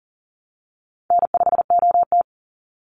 Morse code, heard as a keyed tone, about 40 wpm, 700 Hz. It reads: N5OT
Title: Call Signs Repeated 3x - 40wpm